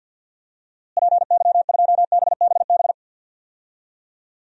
Morse code, heard as a keyed tone, about 52 wpm, 700 Hz. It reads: PY2BBB
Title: DXCC-100 Call Signs - 6 Characters - Repeated 2x - 50wpm